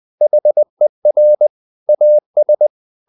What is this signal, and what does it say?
Morse code, heard as a keyed tone, about 20 wpm, 600 Hz. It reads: HER AS
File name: the old man and the sea / the old man and the sea part 1 20wpm